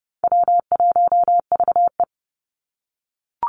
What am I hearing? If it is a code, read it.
Morse code, heard as a keyed tone, about 30 wpm, 700 Hz. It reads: W1VE
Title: Easy Call Signs - 30wpm